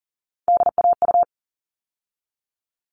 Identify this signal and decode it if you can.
Morse code, heard as a keyed tone, about 40 wpm, 700 Hz. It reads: DAU
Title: All Cut Numbers - 3-Digit 40wpm